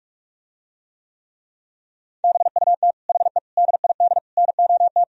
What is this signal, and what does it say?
Morse code, heard as a keyed tone, about 49 wpm, 700 Hz. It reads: BUT HE DID NOT